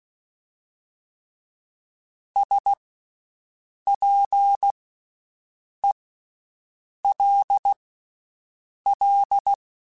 Morse code, heard as a keyed tone, about 16 wpm, spaced as for 6 wpm, 800 Hz. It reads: SPELL
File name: Special INGs / Special INGs-mp3s / VerySlowToFaster Ing @15